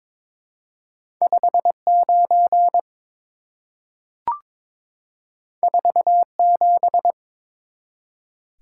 Morse code, heard as a keyed tone, about 22 wpm, 700 Hz. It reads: 59 47
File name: practice / 22wpm / Two Letter-Number Combinations - 22wpm